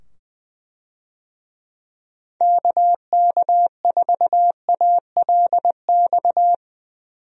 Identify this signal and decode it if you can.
Morse code, heard as a keyed tone, about 20 wpm, 700 Hz. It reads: KK4ALX
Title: USA Call Signs - 6 Characters - 20wpm